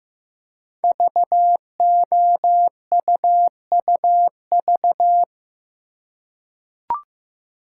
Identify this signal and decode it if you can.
Morse code, heard as a keyed tone, about 15 wpm, 700 Hz. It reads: VOUUV
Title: Code Groups - 5 Letters - 15wpm